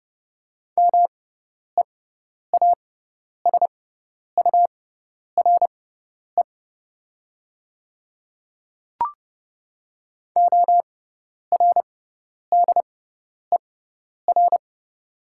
Morse code, heard as a keyed tone, about 29 wpm, spaced as for 10 wpm, 690 Hz. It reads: MEASURE ORDER
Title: 6x Character Spacing - Top 500 Words - 30wpm